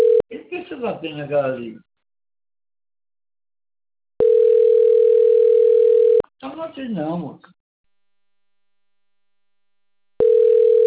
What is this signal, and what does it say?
Ringback tone (North America), 440 + 480 Hz, with speech.
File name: rg-1600-unknown- - - .64294